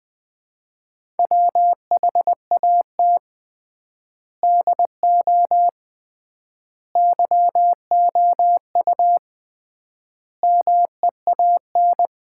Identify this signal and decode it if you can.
Morse code, heard as a keyed tone, about 20 wpm, 700 Hz. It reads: WHAT DO YOU MEAN
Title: 3x Word-Spacing - Sets of 4 Words - 20wpm